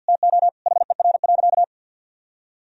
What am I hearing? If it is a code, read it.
Morse code, heard as a keyed tone, about 50 wpm, 700 Hz. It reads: TO HER.
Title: Sentences from Top 1000 Words - Review Part 2 - 50wpm